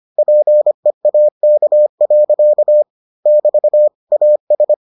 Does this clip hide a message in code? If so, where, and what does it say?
Yes: Morse code from 0.2 to 4.7 s, heard as a keyed tone, about 25 wpm, 600 Hz. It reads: PEAK. <BT> AS